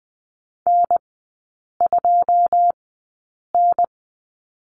Morse code, heard as a keyed tone, about 20 wpm, spaced as for 8 wpm, 700 Hz. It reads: N2N
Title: 152 Lesson 25e - Letter p - Three Characters 20wpm